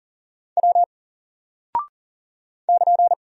Morse code, heard as a keyed tone, about 39 wpm, 700 Hz. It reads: W <KN>
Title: Single Character (Letter:Number:Punctuation:Prosign) 40wpm